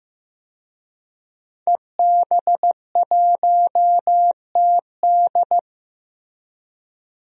Morse code, heard as a keyed tone, about 15 wpm, 700 Hz. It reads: EB1TD
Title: DXCC-100 Call Signs - 5 Characters - 15wpm